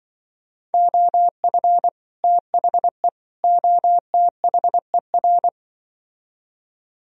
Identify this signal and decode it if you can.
Morse code, heard as a keyed tone, about 24 wpm, 700 Hz. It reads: OF THE OTHER
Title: Speed-Racing - Sets of 3 Words 20wpm